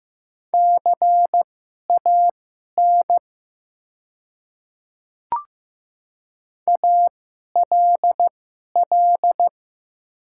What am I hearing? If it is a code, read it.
Morse code, heard as a keyed tone, about 15 wpm, spaced as for 11 wpm, 700 Hz. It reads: CAN ALL
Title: 2x Character Spacing - Top 100 Words - 15wpm